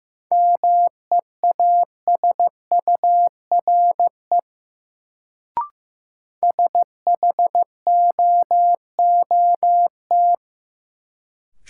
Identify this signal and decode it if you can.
Morse code, heard as a keyed tone, about 15 wpm, 700 Hz. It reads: MEASURE SHOOT
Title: Sentences from Top 1200 Words - New Word Drill Alt - 15wpm